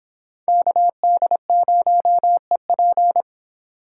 Morse code, heard as a keyed tone, about 30 wpm, 700 Hz. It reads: KD0EP